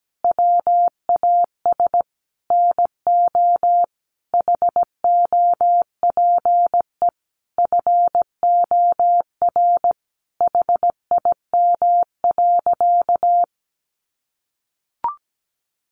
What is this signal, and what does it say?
Morse code, heard as a keyed tone, about 17 wpm, 700 Hz. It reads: WAS NO HOPE FOR HIM.
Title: Sentences from Top 1000 Words 17wpm